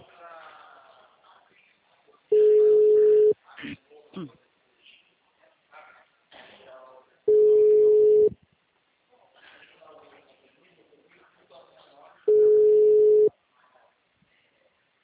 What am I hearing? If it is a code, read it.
Ringback tone (Europe, 425 Hz), with speech.
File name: out- -2008- - - .12